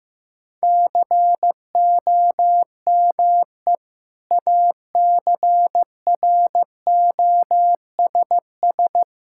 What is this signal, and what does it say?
Morse code, heard as a keyed tone, about 15 wpm, 700 Hz. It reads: COME ACROSS
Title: Sentences from Top 1000 Words - Review Part 5 - 15wpm